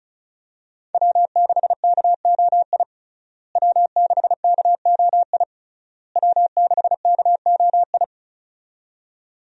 Morse code, heard as a keyed tone, about 36 wpm, 700 Hz. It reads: W6KOI W6KOI W6KOI
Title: USA Call Signs - 5 Characters - Repeated 3x - 35wpm